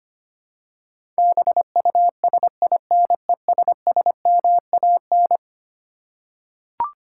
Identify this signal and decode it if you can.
Morse code, heard as a keyed tone, about 25 wpm, 700 Hz. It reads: BUSINESSMAN
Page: morse-code-ninja-practice-25wpm